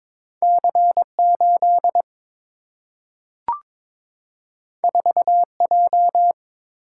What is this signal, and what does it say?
Morse code, heard as a keyed tone, about 22 wpm, 700 Hz. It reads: C 8 4 J
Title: Single-Letter Single-Number 22wpm